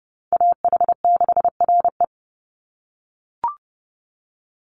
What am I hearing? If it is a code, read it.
Morse code, heard as a keyed tone, about 30 wpm, 700 Hz. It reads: AH6RE
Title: Call Signs - 30wpm